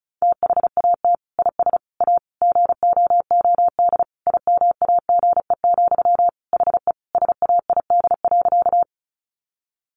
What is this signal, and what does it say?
Morse code, heard as a keyed tone, about 34 wpm, 700 Hz. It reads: THAT IS A GOOD IMAGE, HE SAID.
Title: Sentences from Top 700 Words 35wpm